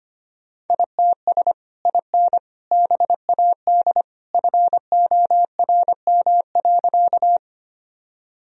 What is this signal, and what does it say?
Morse code, heard as a keyed tone, about 25 wpm, 700 Hz. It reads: ITS IN BAD FORM.
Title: Sentences from Top 1000 Words - Review Part 1 - 25wpm